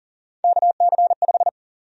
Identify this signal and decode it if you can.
Morse code, heard as a keyed tone, about 40 wpm, 700 Hz. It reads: KC5
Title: USA Call Sign Prefixes - 40wpm